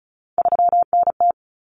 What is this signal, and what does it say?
Morse code, heard as a keyed tone, about 34 wpm, 700 Hz. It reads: 3NT
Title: Common Cut Numbers - 3-Digit 35wpm